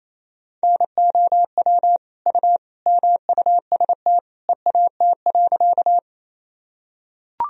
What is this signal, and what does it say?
Morse code, heard as a keyed tone, about 28 wpm, 700 Hz. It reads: NOW U MUST EAT.
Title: Sentences from Top 300 Words and Abbreviations 28wpm